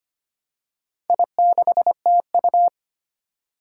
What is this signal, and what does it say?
Morse code, heard as a keyed tone, about 25 wpm, 700 Hz. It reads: I6TU
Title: Four Challenging Letter-Number Combinations 25wpm